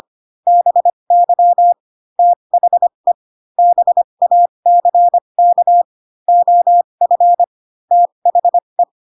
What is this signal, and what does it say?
Morse code, heard as a keyed tone, about 25 wpm, spaced as for 22 wpm, 700 Hz. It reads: BY THE BACK OF THE
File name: Pinocchio / Pinocchio ch3 18 25 ik1hge